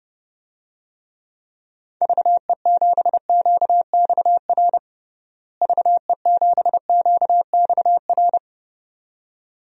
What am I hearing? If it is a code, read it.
Morse code, heard as a keyed tone, about 32 wpm, 700 Hz. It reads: VE7QXR VE7QXR
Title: DXCC-100 Call Signs - 6 Characters - Repeated 2x - 30wpm